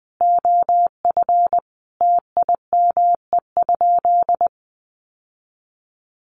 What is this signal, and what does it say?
Morse code, heard as a keyed tone, about 19 wpm, 700 Hz. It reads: OF TIME?
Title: Sentences from Top 1200 Words 20wpm